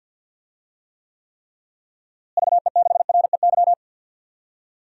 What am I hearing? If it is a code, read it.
Morse code, heard as a keyed tone, about 47 wpm, 700 Hz. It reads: VE6REY
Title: DXCC-100 Call Signs - 6 Characters - Repeated 3x - 50wpm